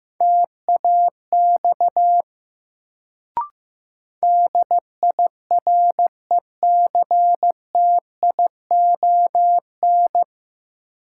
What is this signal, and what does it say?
Morse code, heard as a keyed tone, about 15 wpm, 700 Hz. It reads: TAX DIRECTION
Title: Sentences from Top 1000 Words - Review Part 5 - New Word Drill - 15wpm